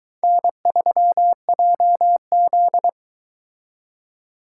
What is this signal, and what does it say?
Morse code, heard as a keyed tone, about 23 wpm, 700 Hz. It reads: N3JZ